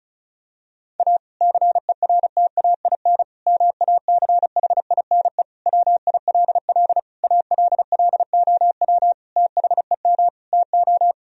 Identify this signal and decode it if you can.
Morse code, heard as a keyed tone, about 36 wpm, 700 Hz. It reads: A CERTAIN MACHINE WILL ALLOW THEM TO
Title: Sentences from Top 1000 Words - Review Part 2 - 35wpm